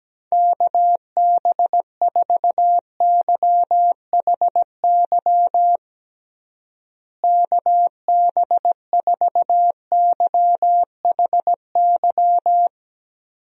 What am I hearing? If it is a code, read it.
Morse code, heard as a keyed tone, about 17 wpm, 700 Hz. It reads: KB4YHY KB4YHY
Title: USA Call Signs - 6 Characters - Repeated 3x - 17wpm